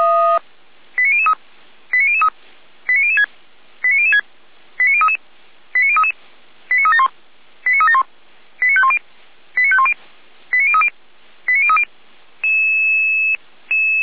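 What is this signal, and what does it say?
Five-tone selective call (ZVEI), heard as quick runs of five tones, tones of about 80 ms. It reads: 89003 89003 89006 89006 89033 89033 89371 89371 89510 89510 89030 89030
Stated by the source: ZVEI five-tone selective call received on an SDR, logged by Feuerwehr Orlamünde (fire brigade).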